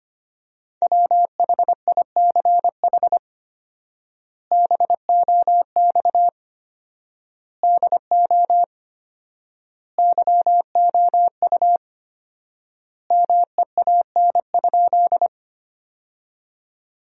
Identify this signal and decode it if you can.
Morse code, heard as a keyed tone, about 25 wpm, 700 Hz. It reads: WHICH BOX DO YOU MEAN?
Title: 4x Word-Spacing - Sentences from Top 500 Words - 25wpm